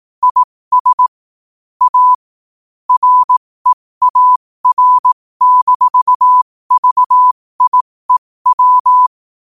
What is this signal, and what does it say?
Morse code, heard as a keyed tone, about 18 wpm, spaced as for 15 wpm, 1000 Hz. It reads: IS A REAR-VIEW
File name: qotdfiles / 15wpm